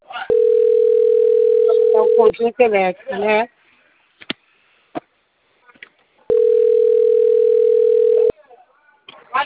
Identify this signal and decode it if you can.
Ringback tone (North America), 440 + 480 Hz, with speech.
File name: out- - - - - .70693